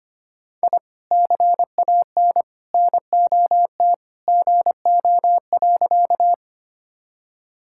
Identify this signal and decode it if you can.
Morse code, heard as a keyed tone, about 25 wpm, 700 Hz. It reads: I CAN NOT GO.
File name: practice / 25wpm / Sentences with 2-Word Phrases - 25wpm